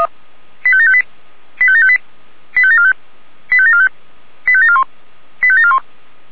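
Five-tone selective call (ZVEI), heard as quick runs of five tones, tones of about 70 ms. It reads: 85758 85758 85745 85745 85741 85741
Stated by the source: ZVEI five-tone selective call received by SDR.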